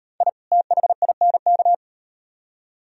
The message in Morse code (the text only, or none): I THINK